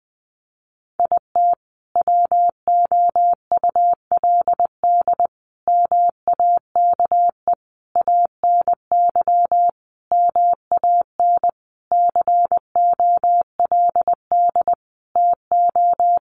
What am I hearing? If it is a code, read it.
Morse code, heard as a keyed tone, about 20 wpm, 700 Hz. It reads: IT WOULD MAKE ANY MAN COLD TO
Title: Sentences from Top 1100 Words 20wpm